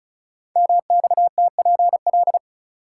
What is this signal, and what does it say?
Morse code, heard as a keyed tone, about 36 wpm, 700 Hz. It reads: MXTPL
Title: Code Groups - 5 Letters - 35wpm